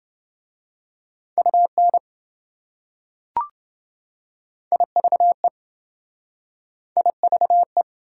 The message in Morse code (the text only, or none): UN IVE IVE